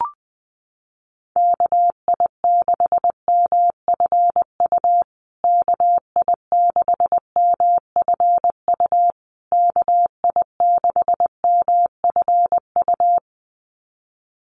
Morse code, heard as a keyed tone, about 20 wpm, 700 Hz. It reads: KI6MFU KI6MFU KI6MFU
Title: USA Call Signs Repeated 3x 20wpm